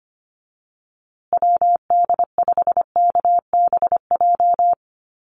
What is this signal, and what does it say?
Morse code, heard as a keyed tone, about 25 wpm, 700 Hz. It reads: WD5KBJ
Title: USA Call Signs Repeated 2x 25wpm